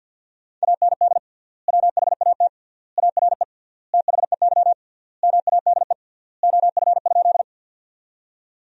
Morse code, heard as a keyed tone, about 50 wpm, 700 Hz. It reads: AND WHAT ARE THEY MADE OF?
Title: 3x Word-Spacing - Sentences from Top 200 Words - 50wpm